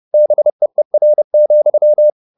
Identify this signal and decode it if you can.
Morse code, heard as a keyed tone, about 30 wpm, 600 Hz. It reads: BEER,
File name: the old man and the sea / the old man and the sea 1 30wpm